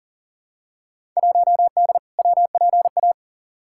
Morse code, heard as a keyed tone, about 40 wpm, 700 Hz. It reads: 1D WPA
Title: Field Day Exchanges - 40wpm